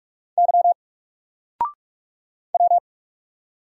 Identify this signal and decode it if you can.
Morse code, heard as a keyed tone, about 44 wpm, 700 Hz. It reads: Y W
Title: Single Letter-Number Alt 45wpm